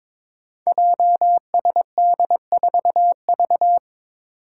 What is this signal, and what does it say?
Morse code, heard as a keyed tone, about 23 wpm, 700 Hz. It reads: JSD4V
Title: Five Challenging Letter-Number Combinations 22wpm